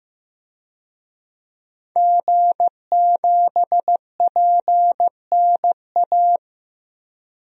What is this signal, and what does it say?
Morse code, heard as a keyed tone, about 15 wpm, 700 Hz. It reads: G7PNA
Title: DXCC-100 Call Signs - 5 Characters - Repeated 3x - 15wpm